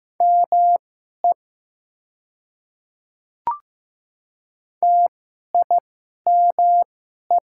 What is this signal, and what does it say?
Morse code, heard as a keyed tone, about 15 wpm, spaced as for 11 wpm, 700 Hz. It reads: ME TIME
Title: 2x Character Spacing - Top 500 Words - 15wpm